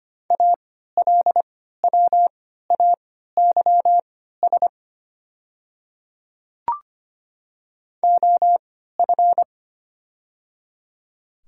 Morse code, heard as a keyed tone, about 25 wpm, spaced as for 14 wpm, 700 Hz. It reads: ALWAYS OF
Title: 3x Character Spacing - Top 300 Words - 25wpm